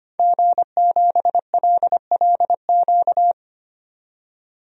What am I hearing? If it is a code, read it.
Morse code, heard as a keyed tone, about 25 wpm, 700 Hz. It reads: G7LLQ
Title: DXCC-100 Call Signs - 5 Characters - 25wpm